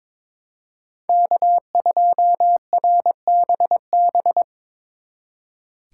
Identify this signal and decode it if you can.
Morse code, heard as a keyed tone, about 25 wpm, 700 Hz. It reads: K2RBB
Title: Speed Racing - USA 1x3 Call Signs 22wpm